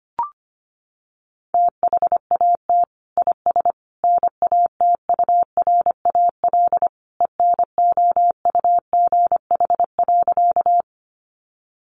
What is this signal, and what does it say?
Morse code, heard as a keyed tone, about 25 wpm, 700 Hz. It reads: THAT IS NATURAL ENOUGH.
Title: Sentences from Top 700 Words 25wpm